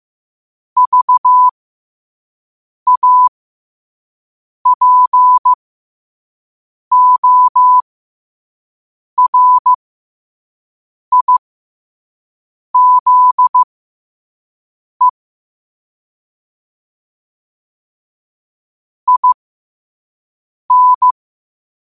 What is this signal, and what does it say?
Morse code, heard as a keyed tone, about 15 wpm, spaced as for 5 wpm, 1000 Hz. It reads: VAPORIZE IN
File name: qotdfiles / 5wpm